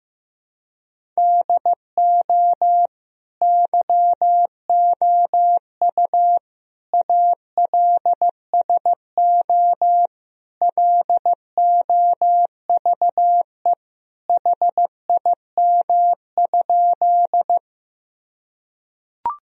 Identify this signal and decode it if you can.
Morse code, heard as a keyed tone, about 16 wpm, 700 Hz. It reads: DO YOU ALSO LOVE HIM?
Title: Sentences from Top 1000 Words - Review Part 1 - 15wpm